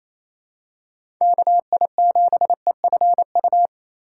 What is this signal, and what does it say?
Morse code, heard as a keyed tone, about 28 wpm, 700 Hz. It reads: KI7EFU
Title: USA Call Signs - 3x Character Spacing - 28wpm